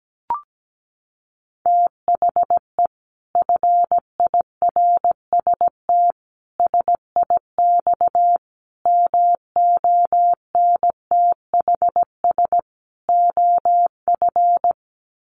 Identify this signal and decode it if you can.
Morse code, heard as a keyed tone, about 17 wpm, 700 Hz. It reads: THE FIRST SIX MONTHS OF
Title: Sets of 5 Words Encore2 17wpm